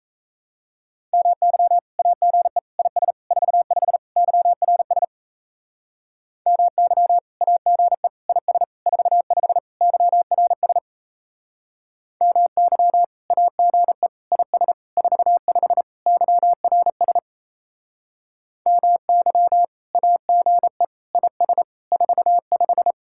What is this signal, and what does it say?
Morse code, heard as a keyed tone, about 34 wpm, 700 Hz. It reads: MY AGE IS 45 YRS MY AGE IS 45 YRS MY AGE IS 45 YRS MY AGE IS 45
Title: Speed-Racing Common QSO Phrases Encore - 28wpm